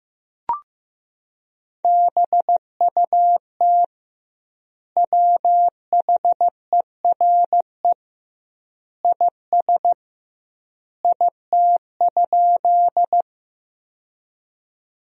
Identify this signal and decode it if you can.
Morse code, heard as a keyed tone, about 15 wpm, 700 Hz. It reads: BUT WHERE IS IT?
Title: 2x Word-Spacing - Sentences from Top 200 Words - 15wpm